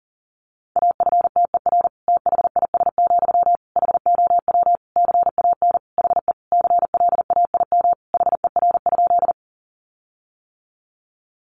Morse code, heard as a keyed tone, about 40 wpm, 700 Hz. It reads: AFTER THIS, HOW CAN HE CLAIM HER?
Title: Sentences from Top 1500 Words 40wpm